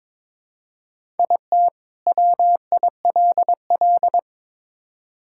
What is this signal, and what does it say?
Morse code, heard as a keyed tone, about 22 wpm, 700 Hz. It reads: IT WILL
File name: Build A Phrase - 2-Word Phrases - 22wpm